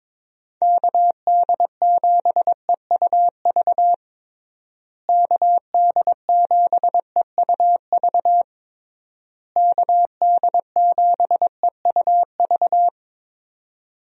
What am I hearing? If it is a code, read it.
Morse code, heard as a keyed tone, about 21 wpm, 700 Hz. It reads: KD7EUV KD7EUV KD7EUV
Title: USA Call Signs - 6 Characters - Repeated 3x - 22wpm